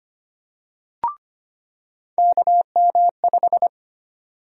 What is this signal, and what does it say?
Morse code, heard as a keyed tone, about 25 wpm, 700 Hz. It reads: KM5
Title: USA Call Sign Prefixes - 25wpm